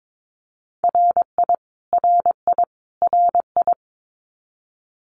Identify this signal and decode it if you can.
Morse code, heard as a keyed tone, about 22 wpm, 700 Hz. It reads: RI RI RI
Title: US States Repeated 3x 22wpm